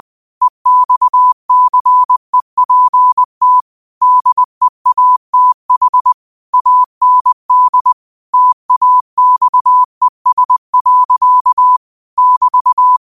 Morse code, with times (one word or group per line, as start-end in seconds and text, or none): 0.4-3.6 EXCEPT
4.0-6.1 DEATH
6.5-7.9 AND
8.3-11.8 TAXES.
12.2-13.0 <BT>